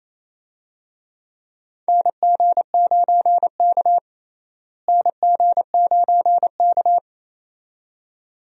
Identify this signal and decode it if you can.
Morse code, heard as a keyed tone, about 28 wpm, 700 Hz. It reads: NG9K NG9K
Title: USA Call Signs - 4 Characters - Repeated 2x - 28wpm